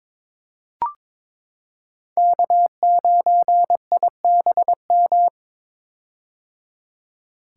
Morse code, heard as a keyed tone, about 22 wpm, 700 Hz. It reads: K9IBM
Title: K1USN SST QSO Practice - 22wpm